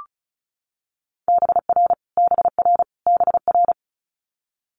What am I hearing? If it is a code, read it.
Morse code, heard as a keyed tone, about 36 wpm, 700 Hz. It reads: BR BR BR
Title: Multi-letter Phonemes Repeated 3x 35wpm